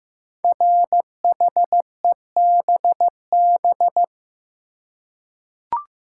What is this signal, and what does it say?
Morse code, heard as a keyed tone, about 15 wpm, 700 Hz. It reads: RHEBB
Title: Code Groups - 5 Letters - 15wpm